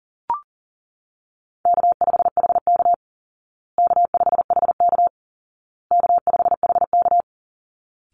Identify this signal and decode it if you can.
Morse code, heard as a keyed tone, about 40 wpm, 700 Hz. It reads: K5HK K5HK K5HK
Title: Call Signs Repeated 2x - 40wpm